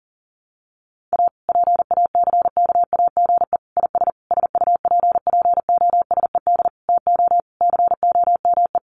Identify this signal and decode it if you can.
Morse code, heard as a keyed tone, about 40 wpm, 700 Hz. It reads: A PACKAGE IS SUPPOSED TO COME
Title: Easy Phrases 40wpm